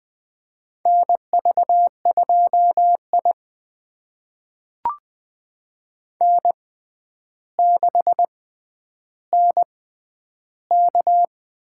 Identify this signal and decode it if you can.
Morse code, heard as a keyed tone, about 20 wpm, spaced as for 10 wpm, 700 Hz. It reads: NV2I N6NK
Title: USA Call Signs - 4 Characters - 6x Character Spacing - 20wpm